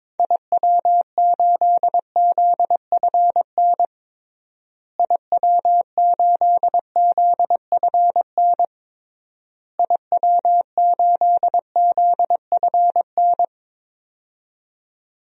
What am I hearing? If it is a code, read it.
Morse code, heard as a keyed tone, about 22 wpm, 700 Hz. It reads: IW8ZFN IW8ZFN IW8ZFN
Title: DXCC-100 Call Signs - 6 Characters - Repeated 3x - 22wpm